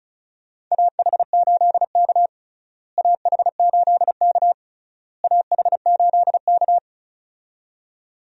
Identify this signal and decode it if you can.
Morse code, heard as a keyed tone, about 36 wpm, 700 Hz. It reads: AH8K AH8K AH8K
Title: USA Call Signs - 4 Characters - Repeated 3x - 35wpm